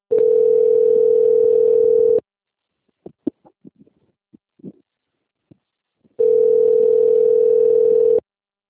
Ringback tone (North America), 440 + 480 Hz.